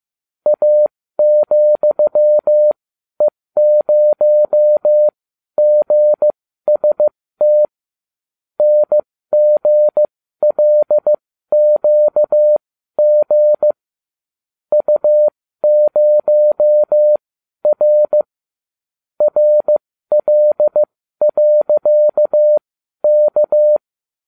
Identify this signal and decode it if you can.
Morse code, heard as a keyed tone, about 15 wpm, spaced as for 12 wpm, 600 Hz. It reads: A,E0GST NGLQG U0R RL.K